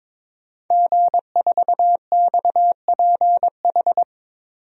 Morse code, heard as a keyed tone, about 22 wpm, 700 Hz. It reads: G4XPH
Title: DXCC-100 Call Signs - 5 Characters - 22wpm